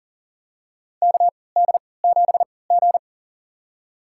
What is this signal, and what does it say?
Morse code, heard as a keyed tone, about 40 wpm, spaced as for 23 wpm, 700 Hz. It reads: KD7G
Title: USA Call Signs - 4 Characters - 3x Character Spacing - 40wpm